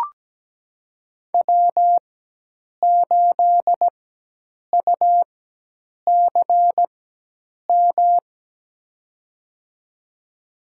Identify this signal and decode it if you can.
Morse code, heard as a keyed tone, about 17 wpm, spaced as for 8 wpm, 700 Hz. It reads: W8UCM
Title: USA Call Signs - 5 Characters - 4x Character Spacing - 17wpm